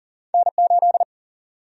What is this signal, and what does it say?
Morse code, heard as a keyed tone, about 40 wpm, 700 Hz. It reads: N8
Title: USA Call Sign Prefixes - 40wpm